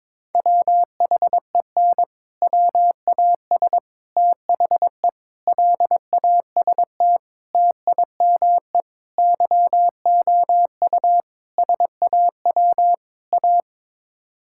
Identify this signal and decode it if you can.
Morse code, heard as a keyed tone, about 22 wpm, 700 Hz. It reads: WHEN WAS THE LAST TIME YOU SAW A
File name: practice / 22wpm / Sets of 8 Words - 22wpm